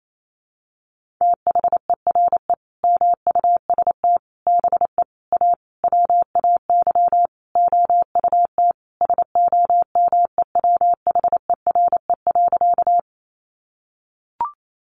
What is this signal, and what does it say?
Morse code, heard as a keyed tone, about 28 wpm, 700 Hz. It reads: THERE MUST BE A WAY OUT SOMEWHERE.
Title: Sentences from Top 1400 Words 28wpm